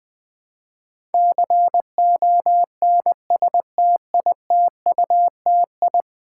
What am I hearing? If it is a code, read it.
Morse code, heard as a keyed tone, about 20 wpm, 700 Hz. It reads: CONSTITUTI
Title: Build A Word - 5 Letters At A Time - 12-Letter Words - 20wpm